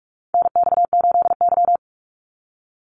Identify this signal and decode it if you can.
Morse code, heard as a keyed tone, about 44 wpm, 700 Hz. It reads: NX8Y
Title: Call Signs Repeated 3x - 45wpm